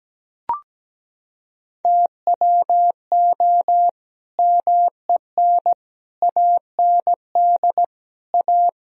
Morse code, heard as a keyed tone, about 17 wpm, 700 Hz. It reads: TWO MEN AND A
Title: Sentences from Top 1000 Words - Review Part 3 - 17wpm